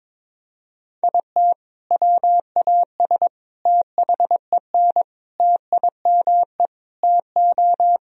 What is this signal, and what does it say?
Morse code, heard as a keyed tone, about 22 wpm, 700 Hz. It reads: IT WAS THEN TIME TO
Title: Sentences from Top 1000 Words - Review Part 1 - 22wpm